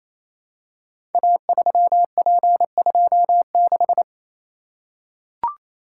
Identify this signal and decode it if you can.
Morse code, heard as a keyed tone, about 29 wpm, 700 Hz. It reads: A3P26
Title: Code Groups - 5 Letter:Numbers - 28wpm